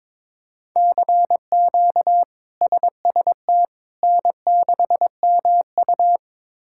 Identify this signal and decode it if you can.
Morse code, heard as a keyed tone, about 22 wpm, 700 Hz. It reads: CQ SST N6MU
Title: K1USN SST QSO Practice - 22wpm